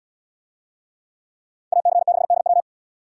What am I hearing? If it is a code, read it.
Morse code, heard as a keyed tone, about 70 wpm, 700 Hz. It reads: AC6DW